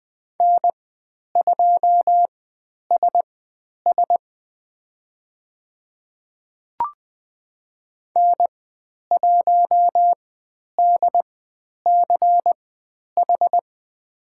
Morse code, heard as a keyed tone, about 20 wpm, spaced as for 10 wpm, 700 Hz. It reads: N2SS N1DCH
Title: 109 Lesson 18g - Letter m - Call Signs 20wpm